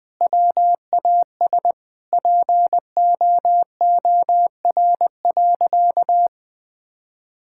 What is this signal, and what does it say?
Morse code, heard as a keyed tone, about 20 wpm, 700 Hz. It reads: WAS POOR.
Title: Sentences from Top 1000 Words - Review Part 5 - 20wpm